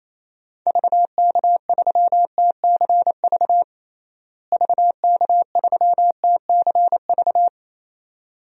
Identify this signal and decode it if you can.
Morse code, heard as a keyed tone, about 27 wpm, 700 Hz. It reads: VK3TCV VK3TCV
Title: DXCC-100 Call Signs - 6 Characters - Repeated 2x - 28wpm